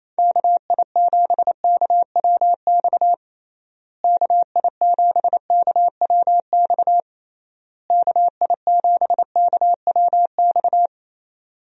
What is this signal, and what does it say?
Morse code, heard as a keyed tone, about 28 wpm, 700 Hz. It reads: KI7KWX KI7KWX KI7KWX
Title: USA Call Signs - 6 Characters - Repeated 3x - 28wpm